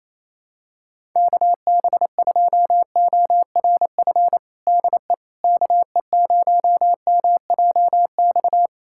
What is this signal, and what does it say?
Morse code, heard as a keyed tone, about 28 wpm, 700 Hz. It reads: KB2ORF DE KE0MJX
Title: USA Callsign DE Callsign - 6 Characters - 28wpm